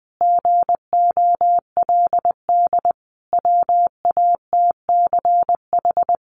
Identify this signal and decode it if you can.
Morse code, heard as a keyed tone, about 20 wpm, 700 Hz. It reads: GOLD WATCH
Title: Sentences from Top 1500 Words 20wpm